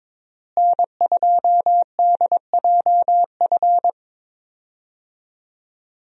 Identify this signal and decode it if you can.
Morse code, heard as a keyed tone, about 22 wpm, 700 Hz. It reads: N2DJF
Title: USA Call Signs - 5 Characters - Repeated 3x - 22wpm